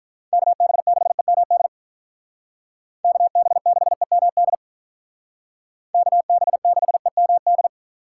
Morse code, heard as a keyed tone, about 44 wpm, 700 Hz. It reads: KB6EMD KB6EMD KB6EMD
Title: Speed Racing - USA 2x3 Call Signs 35wpm